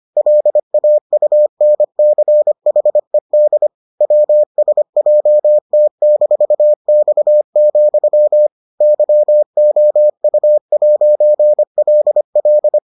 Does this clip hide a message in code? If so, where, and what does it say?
Yes: Morse code from 0.2 to 12.8 s, heard as a keyed tone, about 25 wpm, 600 Hz. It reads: LAUNCHED WSJT-X, YOU'LL